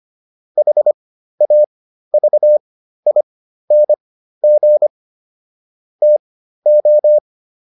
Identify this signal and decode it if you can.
Morse code, heard as a keyed tone, about 25 wpm, spaced as for 13 wpm, 600 Hz. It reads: HAVING TO